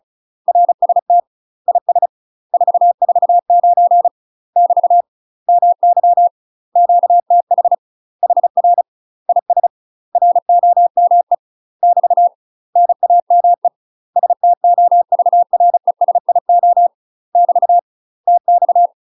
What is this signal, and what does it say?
Morse code, heard as a keyed tone, about 36 wpm, 700 Hz. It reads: RST IS 449 <BT> MY QTH HR IS ROME <BT> NAME STOVRESIO <BT> TX
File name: UK MASTER QSOs TRAINING / 35WPM / UK MASTER QSOs 0014